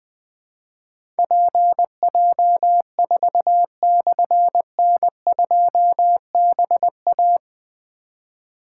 Morse code, heard as a keyed tone, about 20 wpm, 700 Hz. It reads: PJ4/N2BA
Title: Difficult Call Signs 20wpm